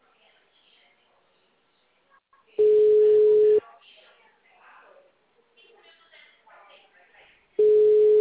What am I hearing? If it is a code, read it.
Ringback tone (Europe, 425 Hz), with speech.
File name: out- -2006- - - .16786